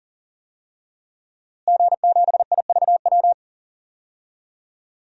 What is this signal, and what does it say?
Morse code, heard as a keyed tone, about 39 wpm, 700 Hz. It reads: G7IVW
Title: DXCC-100 Call Signs - 5 Characters - Repeated 3x - 40wpm